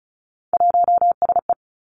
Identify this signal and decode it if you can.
Morse code, heard as a keyed tone, about 36 wpm, 700 Hz. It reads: 1SE
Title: Three Challenging Letter-Number Combinations 35wpm